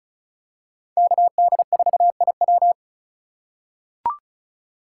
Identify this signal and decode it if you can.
Morse code, heard as a keyed tone, about 36 wpm, 690 Hz. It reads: KD4IW
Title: USA Call Signs - 5 Characters - 3x Character Spacing - 35wpm